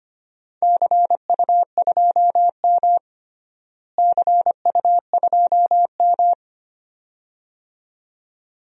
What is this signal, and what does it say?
Morse code, heard as a keyed tone, about 25 wpm, 700 Hz. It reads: CU2M CU2M
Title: DXCC-100 Call Signs - 4 Characters - Repeated 2x - 25wpm